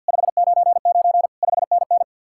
Morse code, heard as a keyed tone, about 65 wpm, 700 Hz. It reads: 599 5NN